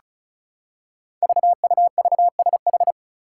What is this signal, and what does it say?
Morse code, heard as a keyed tone, about 36 wpm, 700 Hz. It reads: VUVSH
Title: Five Challenging Letter-Number Combinations 35wpm